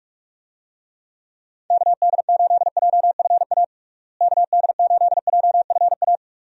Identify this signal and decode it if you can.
Morse code, heard as a keyed tone, about 46 wpm, 700 Hz. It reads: KD8JFA KD8JFA
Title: USA Call Signs - 6 Characters - Repeated 3x - 45wpm